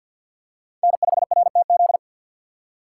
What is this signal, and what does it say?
Morse code, heard as a keyed tone, about 52 wpm, 700 Hz. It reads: N5RTZ
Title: USA Call Signs - 5 Characters - 6x Character Spacing - 50wpm